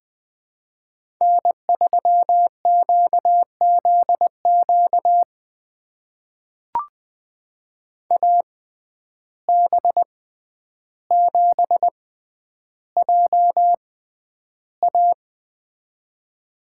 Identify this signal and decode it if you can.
Morse code, heard as a keyed tone, about 20 wpm, spaced as for 10 wpm, 700 Hz. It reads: N3QZQ AB7JA
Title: USA Call Signs - 5 Characters - 6x Character Spacing - 20wpm